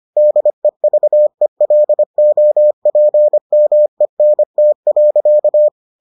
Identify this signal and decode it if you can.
Morse code, heard as a keyed tone, about 25 wpm, 600 Hz. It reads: DEVELOPMENT.